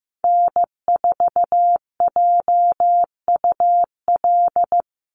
Morse code, heard as a keyed tone, about 15 wpm, 700 Hz. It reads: N4JUL
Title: USA Call Signs 15wpm